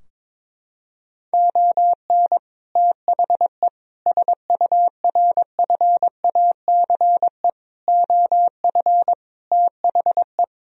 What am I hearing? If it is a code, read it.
Morse code, heard as a keyed tone, about 23 wpm, 700 Hz. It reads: ON THE SURFACE OF THE
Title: Sentences from Top 1000 Words - Review Part 3 - 22wpm